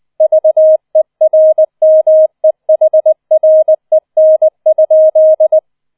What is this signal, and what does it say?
Morse code, heard as a keyed tone, about 20 wpm, 630 Hz. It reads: VERMEHREN?